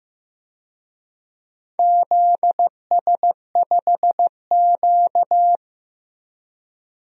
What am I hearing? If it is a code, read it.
Morse code, heard as a keyed tone, about 15 wpm, 700 Hz. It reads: ZS5Q
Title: DXCC-100 Call Signs - 4 Characters - Repeated 2x - 15wpm